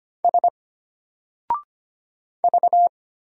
Morse code, heard as a keyed tone, about 25 wpm, 700 Hz. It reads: S V
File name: Single-Letter-Number-25wpm